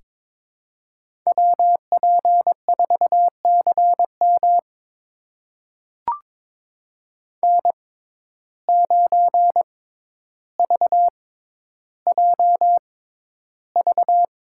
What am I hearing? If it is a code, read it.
Morse code, heard as a keyed tone, about 21 wpm, spaced as for 11 wpm, 700 Hz. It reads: WP4CM N9VJV
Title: USA Call Signs - 5 Characters - 6x Character Spacing - 22wpm